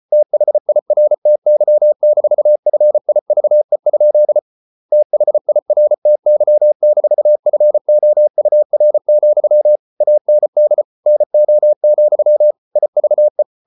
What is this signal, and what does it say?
Morse code, heard as a keyed tone, about 34 wpm, 600 Hz. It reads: THIRTY-FIVE? THIRTY-FOUR, AND NO, IVE